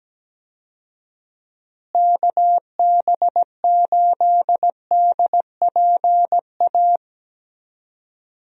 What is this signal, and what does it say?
Morse code, heard as a keyed tone, about 17 wpm, 700 Hz. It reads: KB8DPA
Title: USA Call Signs - 6 Characters - Repeated 2x - 17wpm